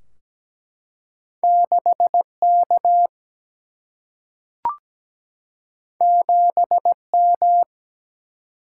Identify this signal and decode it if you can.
Morse code, heard as a keyed tone, about 17 wpm, 700 Hz. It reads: 6 K 7 M
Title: Single-Letter Single-Number 17wpm